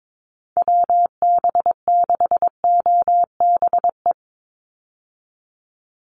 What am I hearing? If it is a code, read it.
Morse code, heard as a keyed tone, about 22 wpm, 700 Hz. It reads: WB6OBE
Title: USA Call Signs 22wpm